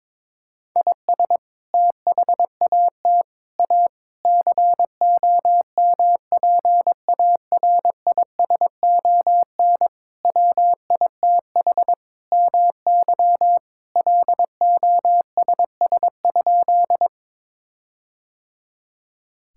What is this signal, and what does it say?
Morse code, heard as a keyed tone, about 22 wpm, 700 Hz. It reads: IS THAT A COMPARISON WITH MY LOSS?
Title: Sentences from Top 1600 Words 22wpm